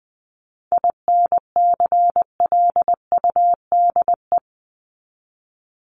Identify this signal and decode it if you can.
Morse code, heard as a keyed tone, about 20 wpm, 700 Hz. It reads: INCLUDE
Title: ED Suffix - Top 100 Words with Base Words 20wpm